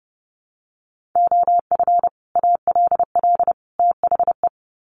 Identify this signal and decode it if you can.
Morse code, heard as a keyed tone, about 30 wpm, 700 Hz. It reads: OF ALL THE
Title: Sets of 3 Words 30wpm